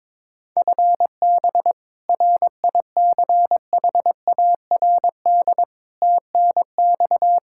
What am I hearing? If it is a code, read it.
Morse code, heard as a keyed tone, about 22 wpm, 700 Hz. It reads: FB RICHARD TNX
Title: Practice QSO - Exchange 3 - More Info - 22wpm